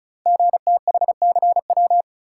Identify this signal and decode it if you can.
Morse code, heard as a keyed tone, about 36 wpm, 700 Hz. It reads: GTHCW
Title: Code Groups - 5 Letters - 35wpm